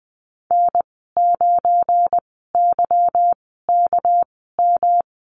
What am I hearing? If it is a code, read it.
Morse code, heard as a keyed tone, about 20 wpm, spaced as for 15 wpm, 700 Hz. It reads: N9YKM
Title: 238 Lesson 38g - Letter q - Call Signs 20wpm